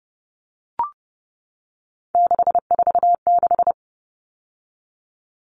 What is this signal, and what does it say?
Morse code, heard as a keyed tone, about 30 wpm, 700 Hz. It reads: 646
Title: Numbers - 3-Digit 30wpm